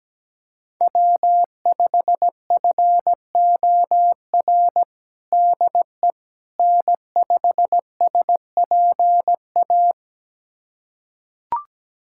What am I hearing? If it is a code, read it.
Morse code, heard as a keyed tone, about 17 wpm, 700 Hz. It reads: W5FOR DE N5SPA
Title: USA Callsign DE Callsign - 5 Characters - 17wpm